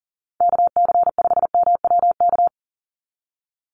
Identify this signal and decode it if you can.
Morse code, heard as a keyed tone, about 40 wpm, 700 Hz. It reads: KC5MWK
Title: USA Call Signs 40wpm